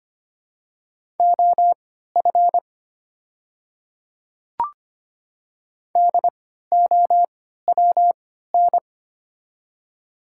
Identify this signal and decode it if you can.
Morse code, heard as a keyed tone, about 25 wpm, spaced as for 14 wpm, 700 Hz. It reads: OF DOWN
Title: 3x Character Spacing - Top 100 Words - 25wpm